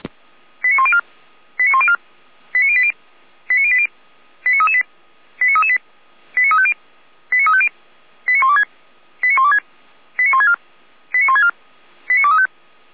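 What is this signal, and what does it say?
Five-tone selective call (ZVEI), heard as quick runs of five tones, tones of about 70 ms. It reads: 89104 89104 89080 89080 89338 89338 89350 89350 89126 89126 89164 89164 89235
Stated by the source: Five-tone ZVEI selective call 89092 received by SDR.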